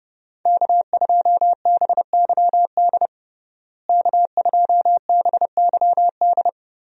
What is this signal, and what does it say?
Morse code, heard as a keyed tone, about 32 wpm, 700 Hz. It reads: K2BYD K2BYD
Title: USA Call Signs - 5 Characters - Repeated 2x - 30wpm